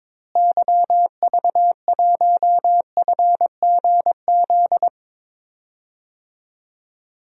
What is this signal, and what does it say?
Morse code, heard as a keyed tone, about 22 wpm, 700 Hz. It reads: YV1FGZ
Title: DXCC-100 Call Signs - 6 Characters - Repeated 3x - 22wpm